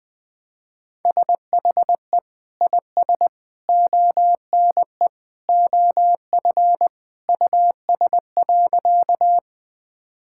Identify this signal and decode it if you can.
Morse code, heard as a keyed tone, about 20 wpm, 700 Hz. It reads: SHE IS ONE OF US.
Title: Sentences with 3-Word Phrases - 20wpm